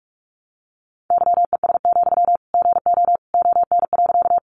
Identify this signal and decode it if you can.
Morse code, heard as a keyed tone, about 44 wpm, 700 Hz. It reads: YES, GO ON.
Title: Sentences from Top 500 Words 45wpm